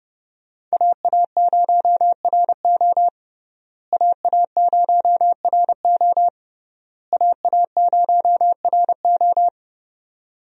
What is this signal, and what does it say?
Morse code, heard as a keyed tone, about 32 wpm, 700 Hz. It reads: AA0RO AA0RO AA0RO
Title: USA Call Signs - 5 Characters - Repeated 3x - 30wpm